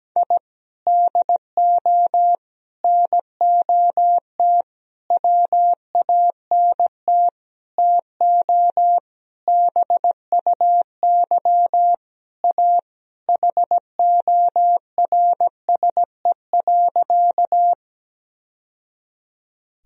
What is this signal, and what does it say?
Morse code, heard as a keyed tone, about 17 wpm, 700 Hz. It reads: I DO NOT WANT TO BUY A HORSE.
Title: Sentences from Top 1000 Words - Review Part 4 - 17wpm